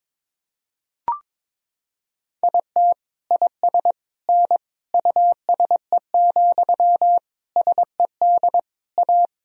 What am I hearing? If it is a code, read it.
Morse code, heard as a keyed tone, about 22 wpm, 700 Hz. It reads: IT IS N USE, SED A
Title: Sentences from Top 500 Words and Abbreviations 22wpm